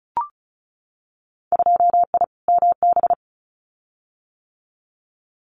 Morse code, heard as a keyed tone, about 36 wpm, 700 Hz. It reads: 2I MB
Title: Winter Field Day Exchanges Encore 35wpm